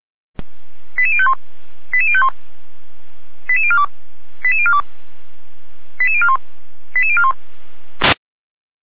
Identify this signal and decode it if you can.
Five-tone selective call (ZVEI), heard as quick runs of five tones, tones of about 80 ms. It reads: 88051 88051 88042 88042 88041 88041
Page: SDR/ZVEI